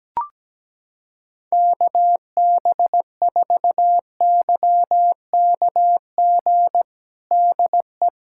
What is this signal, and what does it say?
Morse code, heard as a keyed tone, about 17 wpm, 700 Hz. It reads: KB4YKG DE
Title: USA Callsign DE Callsign 17wpm